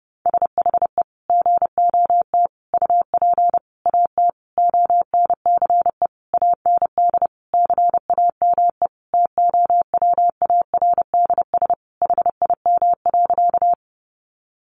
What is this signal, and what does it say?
Morse code, heard as a keyed tone, about 30 wpm, 700 Hz. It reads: SHE GOT UP AT ONCE AND CAME TOWARDS HIM.
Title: Sentences from Top 1100 Words 30wpm